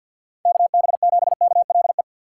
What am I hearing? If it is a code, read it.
Morse code, heard as a keyed tone, about 47 wpm, 700 Hz. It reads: KB7KLE
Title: USA Call Signs - 4x Character Spacing - 50wpm